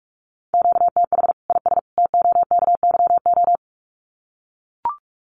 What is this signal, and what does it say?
Morse code, heard as a keyed tone, about 44 wpm, 700 Hz. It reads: QTH IS TOKYO
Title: Common QSO Phrases 45wpm